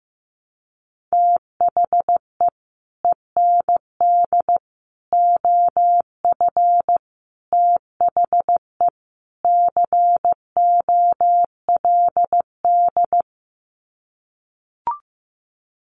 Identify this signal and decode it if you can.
Morse code, heard as a keyed tone, about 15 wpm, 700 Hz. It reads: THE END OF THE COLD
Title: Sets of 5 Words 15wpm